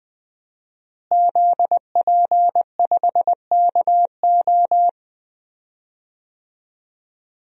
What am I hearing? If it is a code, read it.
Morse code, heard as a keyed tone, about 20 wpm, 700 Hz. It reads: ZP5KO
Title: DXCC-100 Call Signs - 5 Characters - Repeated 3x - 20wpm